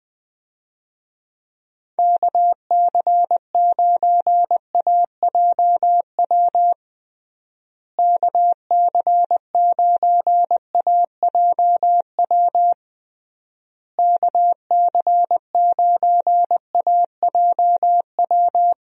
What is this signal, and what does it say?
Morse code, heard as a keyed tone, about 20 wpm, 700 Hz. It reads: KC9AJW KC9AJW KC9AJW
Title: USA Call Signs - 6 Characters - Repeated 3x - 20wpm